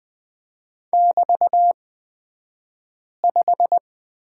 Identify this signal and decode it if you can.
Morse code, heard as a keyed tone, about 20 wpm, spaced as for 5 wpm, 700 Hz. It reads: <BT>5
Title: 223 Lesson 36d - BT - Two Characters 20wpm